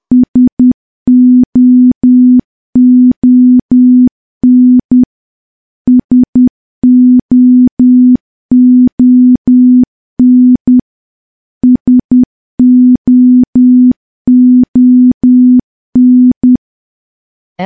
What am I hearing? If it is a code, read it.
Morse code, heard as a keyed tone, about 10 wpm, 250 Hz. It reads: SOON SOON SOON
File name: morse / morse-training-10WPM